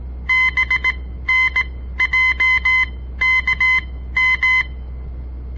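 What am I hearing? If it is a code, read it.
Morse code, heard as a keyed tone, about 17 wpm, spaced as for 13 wpm, 2050 Hz. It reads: BNJKM
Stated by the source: On-air Morse code practice broadcast by a southern Montana amateur radio association; all code, no read back.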